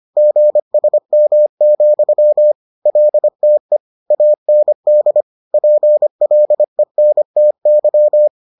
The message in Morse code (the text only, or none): GSM, LTE AND PLENTY